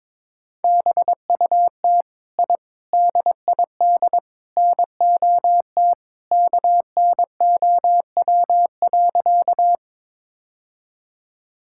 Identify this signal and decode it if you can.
Morse code, heard as a keyed tone, about 22 wpm, 700 Hz. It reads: BUT I DID NOT KNOW.
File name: practice / 22wpm / Sentences with 4-Word Phrases - 22wpm